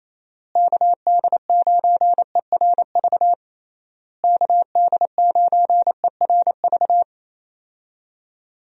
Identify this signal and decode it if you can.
Morse code, heard as a keyed tone, about 28 wpm, 700 Hz. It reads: KD9ERV KD9ERV
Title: USA Call Signs - 6 Characters - Repeated 3x - 28wpm